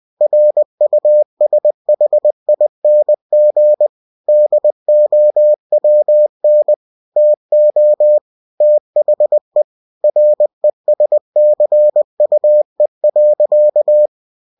Morse code, heard as a keyed tone, about 20 wpm, 600 Hz. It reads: RUSHING DOWN TO THE RESCUE.